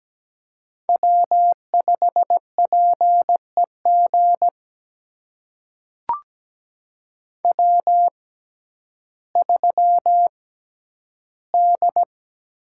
Morse code, heard as a keyed tone, about 17 wpm, spaced as for 10 wpm, 700 Hz. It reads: W5PEG W3D
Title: USA Call Signs - 6x Character Spacing - 17wpm